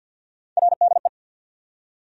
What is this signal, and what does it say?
Morse code, heard as a keyed tone, about 47 wpm, 700 Hz. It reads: RDE
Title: USA Call Sign Postfixes - 3 Letters 50wpm